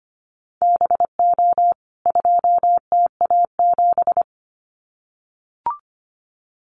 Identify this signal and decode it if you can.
Morse code, heard as a keyed tone, about 25 wpm, 700 Hz. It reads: BO 2TA7
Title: CWT Exchange 25wpm